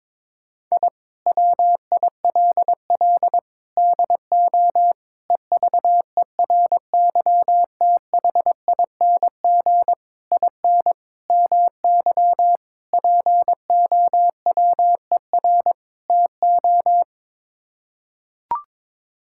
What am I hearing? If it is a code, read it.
Morse code, heard as a keyed tone, about 22 wpm, 700 Hz. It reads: I WILL DO EVERYTHING IN MY POWER TO
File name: practice / 22wpm / Sets of 8 Words - 22wpm